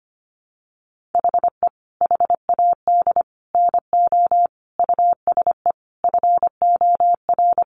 Morse code, heard as a keyed tone, about 25 wpm, 700 Hz. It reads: HE HAD NO USE FOR
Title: Sentences from Top 1200 Words 25wpm